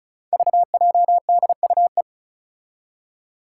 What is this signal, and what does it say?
Morse code, heard as a keyed tone, about 36 wpm, 700 Hz. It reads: VJDUE
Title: Five Challenging Letter-Number Combinations 35wpm